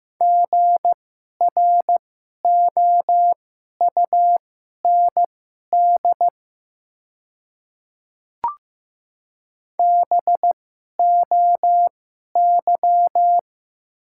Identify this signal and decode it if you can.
Morse code, heard as a keyed tone, about 15 wpm, spaced as for 11 wpm, 700 Hz. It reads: GROUND BOY
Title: 2x Character Spacing - Top 500 Words - 15wpm